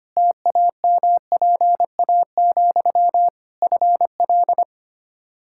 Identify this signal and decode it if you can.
Morse code, heard as a keyed tone, about 25 wpm, 700 Hz. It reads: TAMPA, FL
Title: Practice QSO - Exchange 1 - Beginning - 25wpm